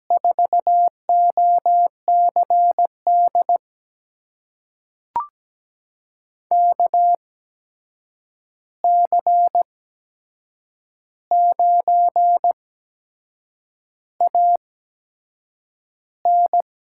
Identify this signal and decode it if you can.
Morse code, heard as a keyed tone, about 17 wpm, spaced as for 7 wpm, 700 Hz. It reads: 4OCD KC9AN
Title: USA Call Signs - 6 Characters - 8x Character Spacing - 17wpm